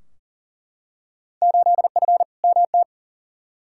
Morse code, heard as a keyed tone, about 40 wpm, 700 Hz. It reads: 8F MT
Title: Field Day Exchanges Encore - 40wpm